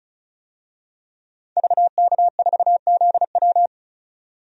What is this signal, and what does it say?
Morse code, heard as a keyed tone, about 34 wpm, 700 Hz. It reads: VK4ZW
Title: DXCC-100 Call Signs - 5 Characters - 35wpm